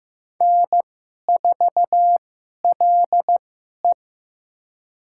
Morse code, heard as a keyed tone, about 15 wpm, spaced as for 11 wpm, 700 Hz. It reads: N4LE
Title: USA Call Signs - 4 Characters - 2x Character Spacing - 15wpm